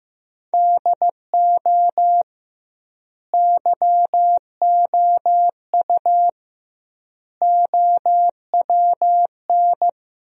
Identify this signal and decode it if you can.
Morse code, heard as a keyed tone, about 15 wpm, 700 Hz. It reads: DO YOU OWN
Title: 2x Word-Spacing - Sentences from Top 400 Words - 15wpm